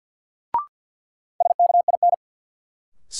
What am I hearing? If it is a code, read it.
Morse code, heard as a keyed tone, about 46 wpm, 700 Hz. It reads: SKIN
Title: Sentences from Top 1300 Words - New Word Drill - 50wpm